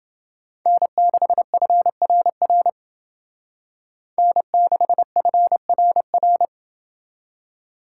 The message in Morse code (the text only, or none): N6FRR N6FRR